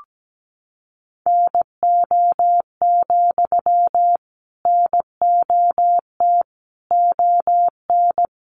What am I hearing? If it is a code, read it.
Morse code, heard as a keyed tone, about 17 wpm, 700 Hz. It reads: NO, NOT ON
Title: Sentences from Top 900 Words 17wpm